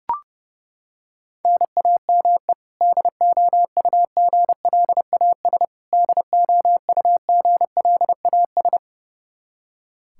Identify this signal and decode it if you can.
Morse code, heard as a keyed tone, about 30 wpm, 700 Hz. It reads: NAME DOUGLAS DOUGLAS
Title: Practice QSO - Exchange 1 - Beginning - 30wpm